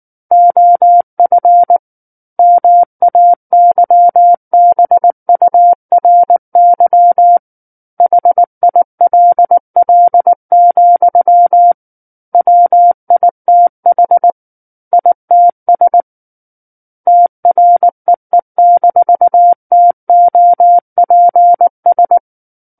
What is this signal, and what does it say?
Morse code, heard as a keyed tone, about 19 wpm, 700 Hz. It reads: OF MAYBURY HILL, WITH ITS TREE-TOPS
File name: Book One / 10-Chapter 10 - 19 WPM